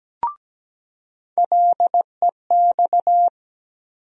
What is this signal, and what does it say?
Morse code, heard as a keyed tone, about 17 wpm, 700 Hz. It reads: LEX
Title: USA Call Sign Postfixes - 3 Letters 17wpm